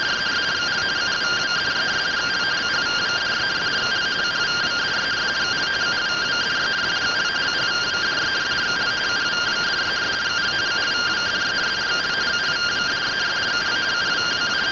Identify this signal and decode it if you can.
Two-tone teleprinter signal, about 170 Hz shift.